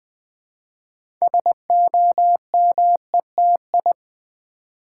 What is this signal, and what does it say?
Morse code, heard as a keyed tone, about 20 wpm, 700 Hz. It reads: SOMETI
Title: Build A Word - 2 Letters At A Time - 8-Letter Words - 20wpm